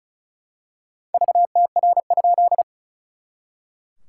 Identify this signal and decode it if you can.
Morse code, heard as a keyed tone, about 34 wpm, 690 Hz. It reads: VTR?
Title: SOTA QSO Practice 35wpm